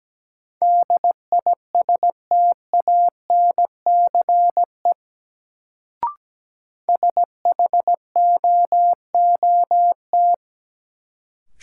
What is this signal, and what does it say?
Morse code, heard as a keyed tone, about 17 wpm, 700 Hz. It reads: DISTANCE SHOOT
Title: Sentences from Top 1200 Words - New Word Drill - 17wpm